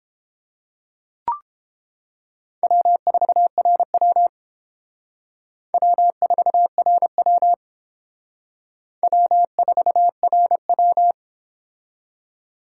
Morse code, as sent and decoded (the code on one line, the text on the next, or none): .-- ....- .-. .-- / .-- ....- .-. .-- / .-- ....- .-. .--
W4RW W4RW W4RW